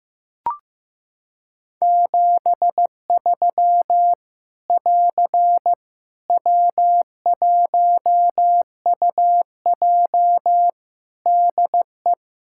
Morse code, heard as a keyed tone, about 15 wpm, 700 Hz. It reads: 73 <AR> W1UJ DE
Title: Practice QSO - Exchange 4 - Ending - 15wpm